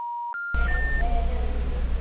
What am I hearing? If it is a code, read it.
Special information tone, with noise.